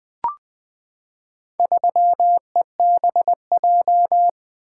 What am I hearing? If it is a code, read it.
Morse code, heard as a keyed tone, about 20 wpm, 700 Hz. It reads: 3EBJ
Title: Four Letter-Number Combinations - 20wpm